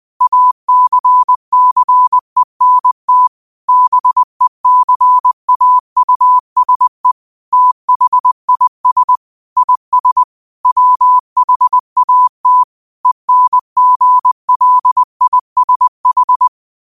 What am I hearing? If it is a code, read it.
Morse code, heard as a keyed tone, about 20 wpm, 1000 Hz. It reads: ACCENT BECAUSE THIS IS WHAT ENGLISH
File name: qotdfiles / 20wpm